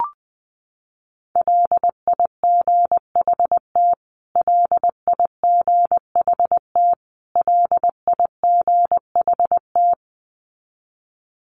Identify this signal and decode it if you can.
Morse code, heard as a keyed tone, about 20 wpm, 700 Hz. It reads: LIGHT LIGHT LIGHT
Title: Top 500 Words Repeated 20wpm